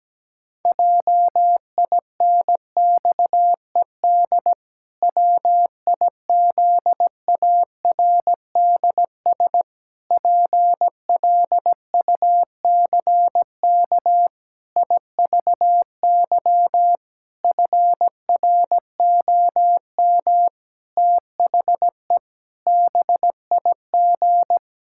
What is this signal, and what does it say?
Morse code, heard as a keyed tone, about 17 wpm, 700 Hz. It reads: JINXED WIZARDS PLUCK IVY FROM THE BIG
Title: Pangram Challenge v2 - 17wpm